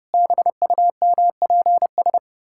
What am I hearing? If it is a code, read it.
Morse code, heard as a keyed tone, about 30 wpm, 700 Hz. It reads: BUMPS
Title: Pangram Challenge v2 - 30wpm